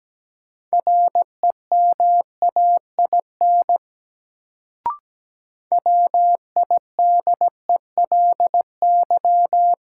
Morse code, heard as a keyed tone, about 17 wpm, 700 Hz. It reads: REMAIN WIDELY
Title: Sentences from Top 1200 Words - New Word Drill Alt - 17wpm